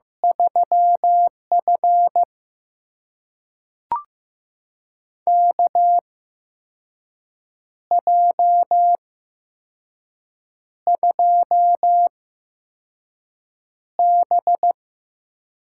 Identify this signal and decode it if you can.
Morse code, heard as a keyed tone, about 15 wpm, spaced as for 5 wpm, 700 Hz. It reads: 3F KJ2B
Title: USA Call Signs - 4 Characters - 8x Character Spacing - 15wpm